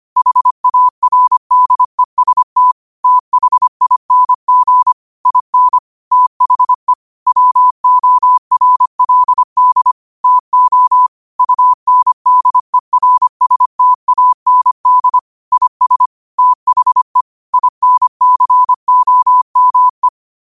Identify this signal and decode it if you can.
Morse code, heard as a keyed tone, about 25 wpm, 1000 Hz. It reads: HARDEST THING IN THE WORLD TO UNDERSTAND IS THE INCOME